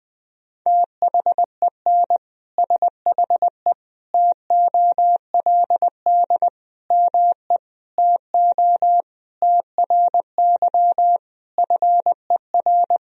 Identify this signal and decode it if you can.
Morse code, heard as a keyed tone, about 20 wpm, 700 Hz. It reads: THEN SHE TOLD ME TO TRY FER
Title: Sentences from Top 400 Words and Abbreviations 20wpm